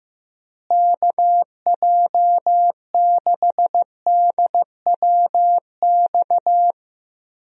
Morse code, heard as a keyed tone, about 15 wpm, 700 Hz. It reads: KJ6DWX
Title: USA Call Signs - 6 Characters - Repeated 3x - 15wpm